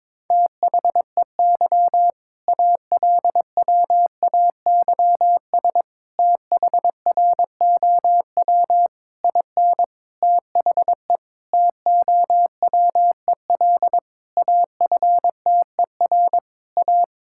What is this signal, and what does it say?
Morse code, heard as a keyed tone, about 22 wpm, 690 Hz. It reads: THEY ALWAYS THROW IN THE TOWEL AFTER A
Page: morse-code-ninja-practice-22wpm